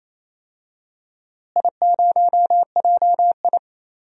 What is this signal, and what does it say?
Morse code, heard as a keyed tone, about 27 wpm, 700 Hz. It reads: I0JI
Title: DXCC-100 Call Signs - 4 Characters - Repeated 3x - 28wpm